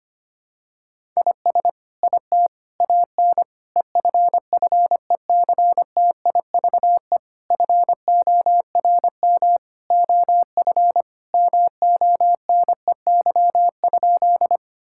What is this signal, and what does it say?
Morse code, heard as a keyed tone, about 25 wpm, 700 Hz. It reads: IS IT AN EFFECTIVE FORM OF MONEY?
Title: Sentences from Top 1000 Words - Review Part 6 - 25wpm